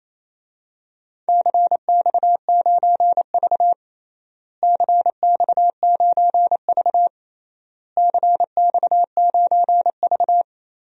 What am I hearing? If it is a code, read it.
Morse code, heard as a keyed tone, about 28 wpm, 700 Hz. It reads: CX9V CX9V CX9V
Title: DXCC-100 Call Signs - 4 Characters - Repeated 3x - 28wpm